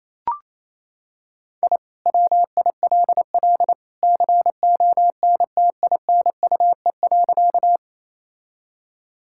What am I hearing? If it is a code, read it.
Morse code, heard as a keyed tone, about 28 wpm, 700 Hz. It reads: I WILL CONTINUE.
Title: Sentences from Top 1000 Words - Review Part 6 - 28wpm